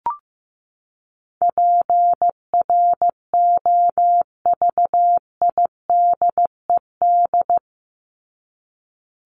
Morse code, heard as a keyed tone, about 15 wpm, 700 Hz. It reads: PROVIDED
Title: Top 1751-2000 Words 15wpm